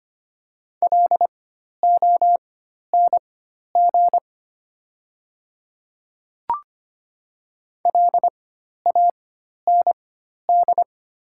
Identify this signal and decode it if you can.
Morse code, heard as a keyed tone, about 25 wpm, spaced as for 12 wpm, 690 Hz. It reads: LONG LAND
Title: 4x Character Spacing - Top 300 Words - 25wpm